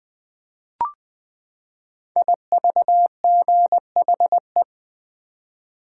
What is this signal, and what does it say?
Morse code, heard as a keyed tone, about 20 wpm, 700 Hz. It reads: IVGHE
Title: Code Groups - 5 Letters - 20wpm